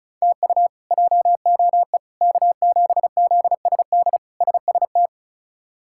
Morse code, heard as a keyed tone, about 34 wpm, 700 Hz. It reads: TU JOE K7ZSD SST
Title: K1USN SST QSO Practice - 35wpm